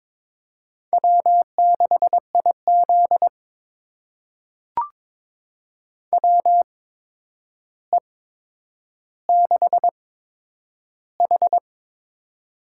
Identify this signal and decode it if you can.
Morse code, heard as a keyed tone, about 21 wpm, spaced as for 9 wpm, 700 Hz. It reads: W6IZ WE6H
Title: USA Call Signs - 4 Characters - 8x Character Spacing - 22wpm